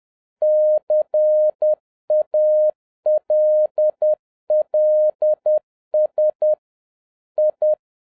Morse code, heard as a keyed tone, about 10 wpm, 610 Hz. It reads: CALLS I